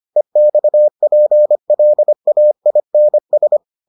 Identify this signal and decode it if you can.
Morse code, heard as a keyed tone, about 25 wpm, 600 Hz. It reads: EXPLAINS